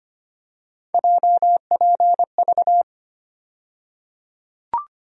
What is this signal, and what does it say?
Morse code, heard as a keyed tone, about 25 wpm, 700 Hz. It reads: JPV
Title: USA Call Sign Postfixes - 3 Letters 25wpm